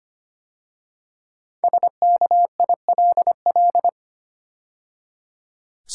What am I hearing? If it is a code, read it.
Morse code, heard as a keyed tone, about 25 wpm, 700 Hz. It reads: SKILL
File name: 5 Letter Words - First 3 Letters Given - 25wpm